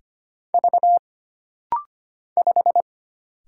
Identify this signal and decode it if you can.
Morse code, heard as a keyed tone, about 25 wpm, 700 Hz. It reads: V 5
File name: Single Character (Letter:Number:Punctuation:Prosign) - Rapid-Fire - 25wpm